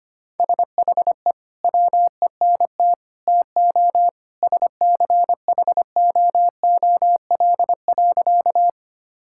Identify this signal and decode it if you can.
Morse code, heard as a keyed tone, about 25 wpm, 700 Hz. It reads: SHE WENT TO SCHOOL.
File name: practice / 25wpm / Sentences with 2-Word Phrases - 25wpm